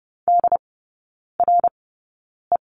Morse code, heard as a keyed tone, about 30 wpm, spaced as for 9 wpm, 700 Hz. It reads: DRE
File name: Lesson 19e - Letter w - Three Characters 30wpm